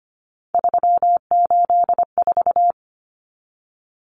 Morse code, heard as a keyed tone, about 25 wpm, 700 Hz. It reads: 384
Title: Common Cut Numbers - 3-Digit 25wpm